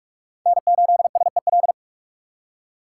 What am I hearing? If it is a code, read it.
Morse code, heard as a keyed tone, about 41 wpm, 700 Hz. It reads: N8SEL
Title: USA Call Signs - 5 Characters - 3x Character Spacing - 45wpm